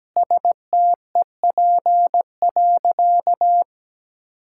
Morse code, heard as a keyed tone, about 17 wpm, 700 Hz. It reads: STEP.
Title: Sentences from Top 1600 Words 17wpm